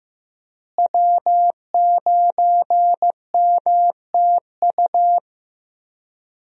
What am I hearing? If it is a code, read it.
Morse code, heard as a keyed tone, about 15 wpm, 700 Hz. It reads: W9MTU
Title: USA Call Signs - 5 Characters - Repeated 3x - 15wpm